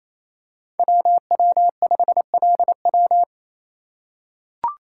Morse code, heard as a keyed tone, about 28 wpm, 700 Hz. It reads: WW5LW
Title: USA Call Signs - 5 Characters - 2x Character Spacing - 28wpm